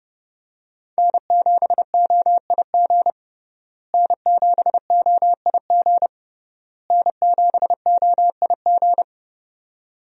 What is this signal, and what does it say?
Morse code, heard as a keyed tone, about 31 wpm, 700 Hz. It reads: N7OIG N7OIG N7OIG
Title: USA Call Signs - 5 Characters - Repeated 3x - 30wpm